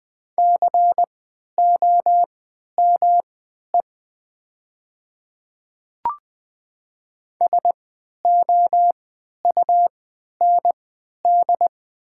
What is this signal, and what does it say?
Morse code, heard as a keyed tone, about 20 wpm, spaced as for 11 wpm, 700 Hz. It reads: COME SOUND
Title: 3x Character Spacing - Top 100 Words - 20wpm